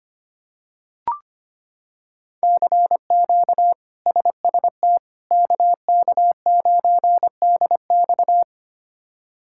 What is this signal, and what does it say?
Morse code, heard as a keyed tone, about 25 wpm, 700 Hz. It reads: CQ SST KK9DX
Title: K1USN SST QSO Practice - 25wpm